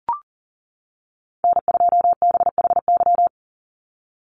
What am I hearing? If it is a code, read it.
Morse code, heard as a keyed tone, about 40 wpm, 700 Hz. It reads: N2BHY
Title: USA Call Signs 40wpm